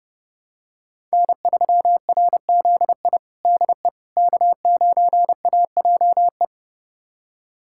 Morse code, heard as a keyed tone, about 30 wpm, 700 Hz. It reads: N3RZI DE K9AJE
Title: USA Callsign DE Callsign - 5 Characters - 30wpm